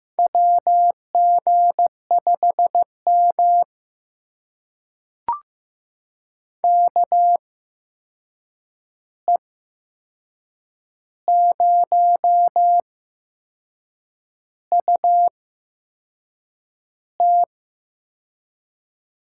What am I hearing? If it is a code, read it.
Morse code, heard as a keyed tone, about 15 wpm, spaced as for 6 wpm, 700 Hz. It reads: WG5M KE0UT
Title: USA Call Signs - 8x Character Spacing - 15wpm